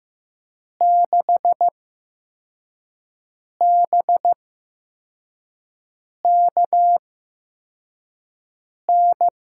Morse code, heard as a keyed tone, about 15 wpm, spaced as for 4 wpm, 700 Hz. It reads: 6BKN
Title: USA Call Signs - 6 Characters - 8x Character Spacing - 15wpm